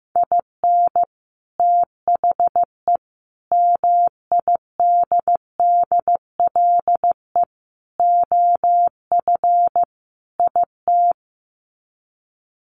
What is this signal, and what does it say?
Morse code, heard as a keyed tone, about 15 wpm, 700 Hz. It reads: IN THE MIDDLE OF IT
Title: Sets of 5 Words Encore 15wpm